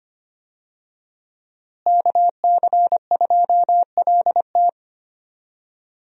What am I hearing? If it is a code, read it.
Morse code, heard as a keyed tone, about 25 wpm, 700 Hz. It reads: KC2LT
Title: USA Call Signs - 5 Characters - Repeated 2x - 25wpm